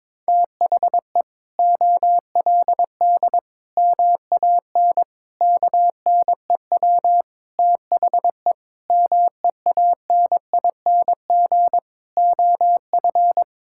Morse code, heard as a keyed tone, about 22 wpm, 700 Hz. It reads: THE OLD MAN KNEW THE MEANING OF
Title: Sentences from Top 1000 Words - Review Part 6 - 22wpm